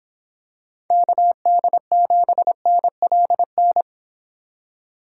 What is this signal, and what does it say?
Morse code, heard as a keyed tone, about 27 wpm, 700 Hz. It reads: KD7NLN
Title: Speed Racing - USA 2x3 Call Signs 22wpm